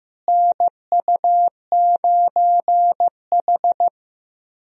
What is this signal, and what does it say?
Morse code, heard as a keyed tone, about 15 wpm, 700 Hz. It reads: NU9H
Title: USA Call Signs - 4 Characters - Repeated 3x - 15wpm